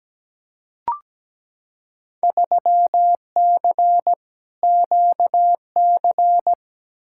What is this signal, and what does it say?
Morse code, heard as a keyed tone, about 17 wpm, 700 Hz. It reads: 3C QC
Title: Field Day Exchanges Encore - 17wpm